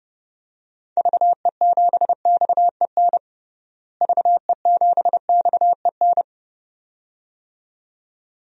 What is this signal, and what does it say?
Morse code, heard as a keyed tone, about 30 wpm, 700 Hz. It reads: VE7XEN VE7XEN
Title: DXCC-100 Call Signs - 6 Characters - Repeated 2x - 30wpm